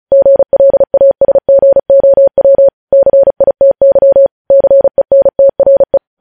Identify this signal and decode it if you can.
Morse code, heard as a keyed tone, about 36 wpm, 550 Hz. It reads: GLASGOW CITY CENTRE
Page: news headlines in morse code at 35 wpm